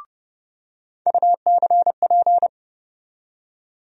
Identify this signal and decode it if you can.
Morse code, heard as a keyed tone, about 30 wpm, 700 Hz. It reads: UCP
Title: USA Call Sign Postfixes - 3 Letters 30wpm